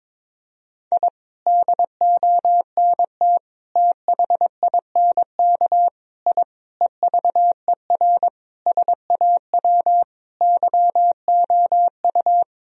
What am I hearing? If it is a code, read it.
Morse code, heard as a keyed tone, about 22 wpm, 700 Hz. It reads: I DONT THINK I EVER SAW YOU
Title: Sentences from Top 1000 Words - Review Part 4 - 22wpm